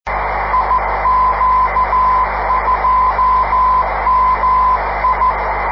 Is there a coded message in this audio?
Yes: Morse code from 0.5 to 5.3 s, heard as a keyed tone, about 14 wpm, 1020 Hz, with noise. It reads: IQ2MI